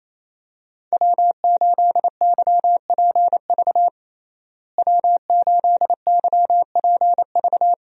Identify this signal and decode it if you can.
Morse code, heard as a keyed tone, about 28 wpm, 700 Hz. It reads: W8YPV W8YPV
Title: USA Call Signs - 5 Characters - Repeated 2x - 28wpm